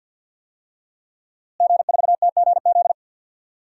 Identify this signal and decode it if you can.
Morse code, heard as a keyed tone, about 47 wpm, 700 Hz. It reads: G4TGZ
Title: DXCC-100 Call Signs - 5 Characters - Repeated 2x - 50wpm